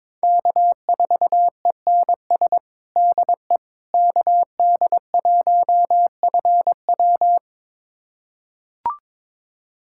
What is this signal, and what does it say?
Morse code, heard as a keyed tone, about 21 wpm, 700 Hz. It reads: K4ENS DE KD1FW
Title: USA Callsign DE Callsign 22wpm